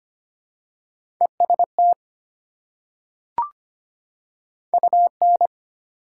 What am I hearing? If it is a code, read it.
Morse code, heard as a keyed tone, about 25 wpm, 690 Hz. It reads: EST UN
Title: Prefixes and Suffixes 25wpm